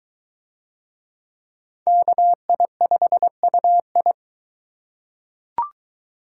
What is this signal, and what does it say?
Morse code, heard as a keyed tone, about 22 wpm, 700 Hz. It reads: KI5UI